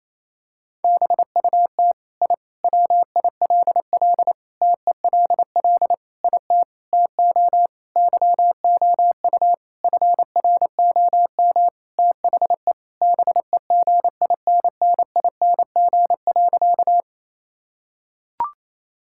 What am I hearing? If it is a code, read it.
Morse code, heard as a keyed tone, about 28 wpm, 700 Hz. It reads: BUT I WILL TELL IT TO YOU FROM THE BEGINNING.
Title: Sentences from Top 1000 Words - Review Part 5 - 28wpm